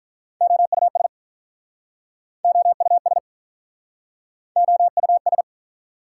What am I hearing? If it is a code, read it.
Morse code, heard as a keyed tone, about 47 wpm, 700 Hz. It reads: OUS OUS OUS